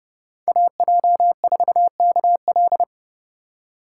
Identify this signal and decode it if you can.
Morse code, heard as a keyed tone, about 30 wpm, 700 Hz. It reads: AJ4KL
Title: USA Call Signs - 5 Characters - 2x Character Spacing - 30wpm